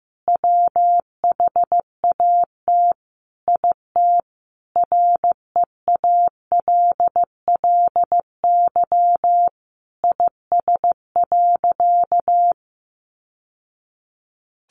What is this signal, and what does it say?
Morse code, heard as a keyed tone, about 15 wpm, 700 Hz. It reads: WHAT IT REALLY IS.
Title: Sentences from Top 600 Words 15wpm